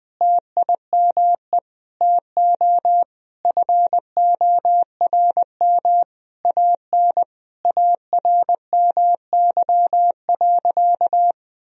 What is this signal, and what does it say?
Morse code, heard as a keyed tone, about 20 wpm, 700 Hz. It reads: TIME TO FORM AN ARMY.
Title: Sentences from Top 1000 Words - Review Part 1 - 20wpm